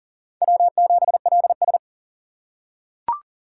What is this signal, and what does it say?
Morse code, heard as a keyed tone, about 40 wpm, 700 Hz. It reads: W7LS
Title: USA Call Signs - 4x Character Spacing - 40wpm